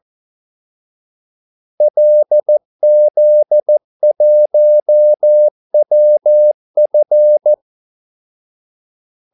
Morse code, heard as a keyed tone, about 14 wpm, 600 Hz. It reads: LZ1WF
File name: Callsigns-14wpm